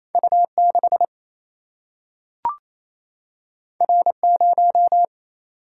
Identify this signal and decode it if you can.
Morse code, heard as a keyed tone, about 28 wpm, 700 Hz. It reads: U 6 R 0
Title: Single-Letter Single-Number 28wpm